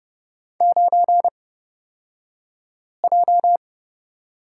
Morse code, heard as a keyed tone, about 30 wpm, spaced as for 5 wpm, 700 Hz. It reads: 9J
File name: Lesson 37d - Letter x - Two Characters 30wpm